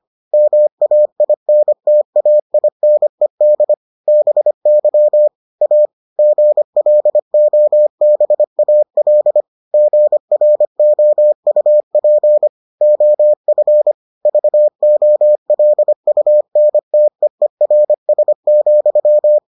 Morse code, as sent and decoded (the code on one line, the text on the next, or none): -- .- .. -. - .- .. -. -.. / -... -.-- / .- / --. .-.. --- -... .- .-.. / --. .-. --- ..- .--. / --- ..-. / ...- --- .-.. ..- -. - . .-. ... --..--
MAINTAINED BY A GLOBAL GROUP OF VOLUNTEERS,